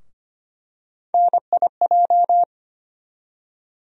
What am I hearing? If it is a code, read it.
Morse code, heard as a keyed tone, about 25 wpm, 700 Hz. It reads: NIJ
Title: USA Call Sign Postfixes - 3 Letters 25wpm